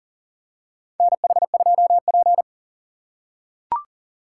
Morse code, heard as a keyed tone, about 40 wpm, 700 Hz. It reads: NH2P
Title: USA Call Signs - 4 Characters - 3x Character Spacing - 40wpm